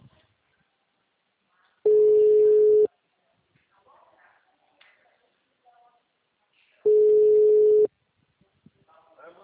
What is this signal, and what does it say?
Ringback tone (Europe, 425 Hz), with speech.
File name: out- -2007- - - .14745